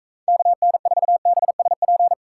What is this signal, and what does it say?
Morse code, heard as a keyed tone, about 44 wpm, 700 Hz. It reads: KN4BSP